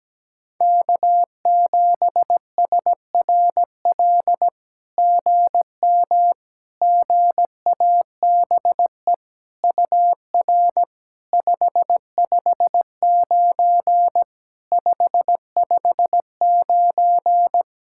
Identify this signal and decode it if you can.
Morse code, heard as a keyed tone, about 17 wpm, 700 Hz. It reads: K7SRL GM GABE UR 559 559
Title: SOTA QSO Practice 17wpm